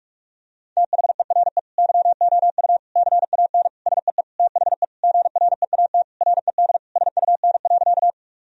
Morse code, heard as a keyed tone, about 46 wpm, 700 Hz. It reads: THERE YOU CAN SEE THE GREAT RED SUN.
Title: Sentences from Top 1000 Words - Review Part 2 - 45wpm